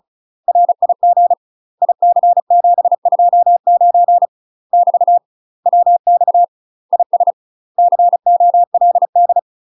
Morse code, heard as a keyed tone, about 34 wpm, 700 Hz. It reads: RIG IC729 <BT> WX IS COLD
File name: UK MASTER QSOs TRAINING / 35WPM / UK MASTER QSOs 0038